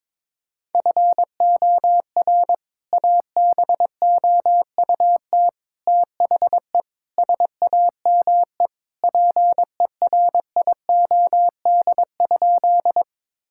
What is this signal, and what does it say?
Morse code, heard as a keyed tone, about 22 wpm, 700 Hz. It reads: FOR ABOUT THE SAME PERIOD?
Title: Sentences from Top 1000 Words - Review Part 5 - 22wpm